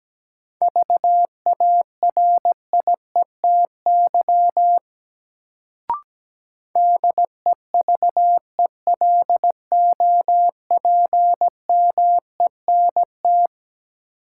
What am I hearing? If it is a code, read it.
Morse code, heard as a keyed tone, about 17 wpm, 700 Hz. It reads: VARIETY DEVELOPMENT
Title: Sentences from Top 1000 Words - Review Part 4 - New Word Drill - 17wpm